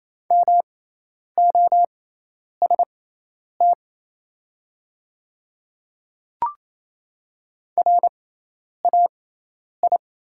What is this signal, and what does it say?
Morse code, heard as a keyed tone, about 27 wpm, spaced as for 10 wpm, 690 Hz. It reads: MOST RAI
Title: 6x Character Spacing - Top 500 Words - 28wpm